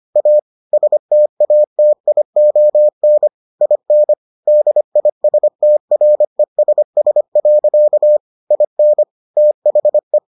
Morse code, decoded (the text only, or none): A STATION IN DISTRESS. IN THE